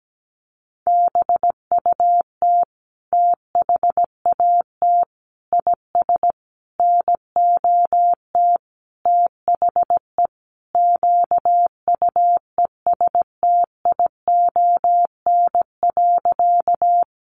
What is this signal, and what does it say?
Morse code, heard as a keyed tone, about 17 wpm, 700 Hz. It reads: BUT THAT IS NOT THE QUESTION.
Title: Sentences from Top 600 Words 17wpm